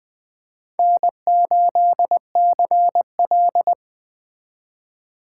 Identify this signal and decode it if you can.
Morse code, heard as a keyed tone, about 20 wpm, 700 Hz. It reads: N8CL
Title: Speed Racing - USA Call Signs 17wpm